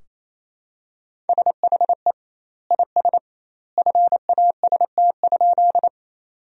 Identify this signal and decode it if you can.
Morse code, heard as a keyed tone, about 28 wpm, 700 Hz. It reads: SHE IS FAST?
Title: 2x Word-Spacing - Sentences from Top 400 Words - 28wpm